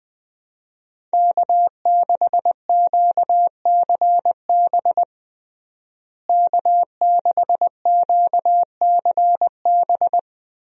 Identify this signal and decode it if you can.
Morse code, heard as a keyed tone, about 20 wpm, 700 Hz. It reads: K6QCB K6QCB
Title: USA Call Signs - 5 Characters - Repeated 2x - 20wpm